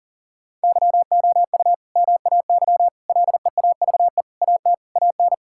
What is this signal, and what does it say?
Morse code, heard as a keyed tone, about 40 wpm, 700 Hz. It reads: YOU MAY LEAVE AT AN
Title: Sentences from Top 1000 Words - Review Part 5 - 40wpm